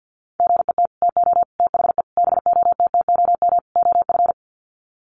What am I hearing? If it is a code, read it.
Morse code, heard as a keyed tone, about 50 wpm, 700 Hz. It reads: GET TO THE BOTTOM OF
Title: Sets of 5 Words Encore 50wpm